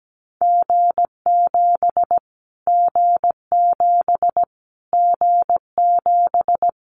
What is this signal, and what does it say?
Morse code, heard as a keyed tone, about 17 wpm, 700 Hz. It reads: G7 G7 G7
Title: Callsign Prefixes for Top 25 DXCC - Repeated 3x - 17wpm